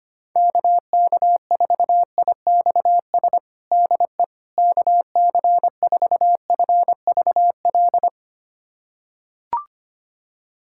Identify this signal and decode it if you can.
Morse code, heard as a keyed tone, about 25 wpm, 700 Hz. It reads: KK4IXS DE KC4FVL
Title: USA Callsign DE Callsign - 6 Characters - 25wpm